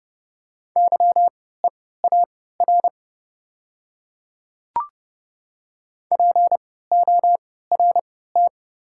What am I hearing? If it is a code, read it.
Morse code, heard as a keyed tone, about 31 wpm, spaced as for 17 wpm, 700 Hz. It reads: YEAR PORT
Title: 3x Character Spacing - Top 200 Words - 30wpm